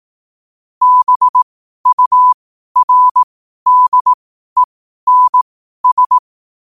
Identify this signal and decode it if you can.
Morse code, heard as a keyed tone, about 18 wpm, spaced as for 12 wpm, 1000 Hz. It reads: BURDENS
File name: qotdfiles / 12wpm